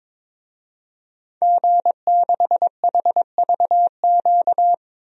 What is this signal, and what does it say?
Morse code, heard as a keyed tone, about 22 wpm, 700 Hz. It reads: G6HVQ
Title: DXCC-100 Call Signs - 5 Characters - 22wpm